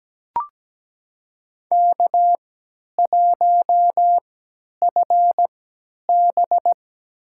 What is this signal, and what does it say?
Morse code, heard as a keyed tone, about 17 wpm, spaced as for 10 wpm, 700 Hz. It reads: K1FB
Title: USA Call Signs - 4 Characters - 3x Character Spacing - 17wpm